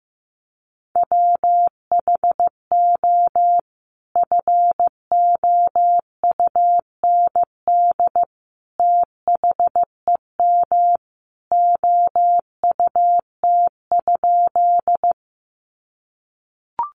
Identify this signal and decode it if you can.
Morse code, heard as a keyed tone, about 15 wpm, 700 Hz. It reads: WHO FOUND THEM OUT?
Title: Sentences from Top 300 Words 15wpm